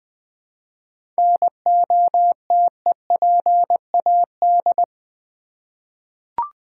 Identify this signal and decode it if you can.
Morse code, heard as a keyed tone, about 20 wpm, 700 Hz. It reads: NOTEPAD
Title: Easy Words with Hints - 20wpm